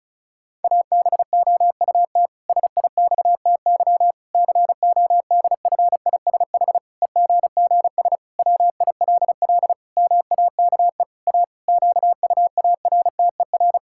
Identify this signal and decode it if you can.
Morse code, heard as a keyed tone, about 36 wpm, 700 Hz. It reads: ABOUT SIXTY CODFISH EGGS WILL MAKE A QUARTER
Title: Pangram Challenge v2 - 35wpm